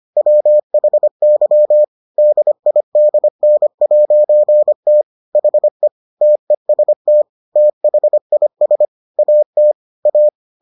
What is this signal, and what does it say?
Morse code, heard as a keyed tone, about 25 wpm, 600 Hz. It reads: WHY DIDN'T HE TEST THIS AT A